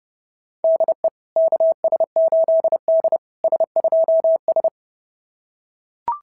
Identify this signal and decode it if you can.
Morse code, heard as a keyed tone, about 28 wpm, 690 Hz. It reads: DE KS8D S2S
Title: SOTA QSO Practice 30wpm